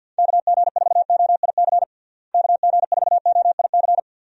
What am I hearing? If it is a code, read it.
Morse code, heard as a keyed tone, about 51 wpm, 700 Hz. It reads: KG4OIC KG4OIC
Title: USA Call Signs - 6 Characters - Repeated 2x - 50wpm